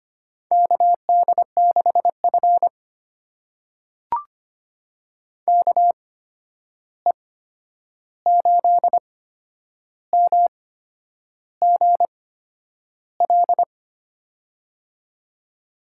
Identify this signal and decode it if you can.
Morse code, heard as a keyed tone, about 25 wpm, spaced as for 9 wpm, 700 Hz. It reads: KD6F KE8MGL
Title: USA Call Signs - 8x Character Spacing - 25wpm